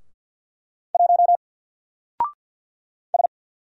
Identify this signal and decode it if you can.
Morse code, heard as a keyed tone, about 50 wpm, 700 Hz. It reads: 1 S
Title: Single Character (Letter:Number:Punctuation:Prosign) 50wpm